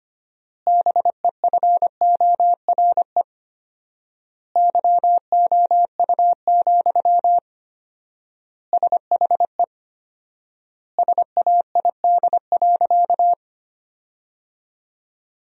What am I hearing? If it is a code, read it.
Morse code, heard as a keyed tone, about 25 wpm, 700 Hz. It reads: BEFORE YOU, SHE SAID.
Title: 4x Word-Spacing - Sentences from Top 400 Words - 25wpm